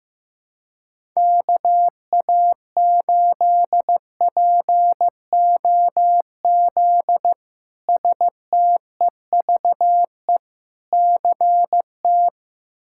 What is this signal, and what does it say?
Morse code, heard as a keyed tone, about 15 wpm, 700 Hz. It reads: KA8POZ STEVE CT
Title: K1USN SST QSO Practice - 15wpm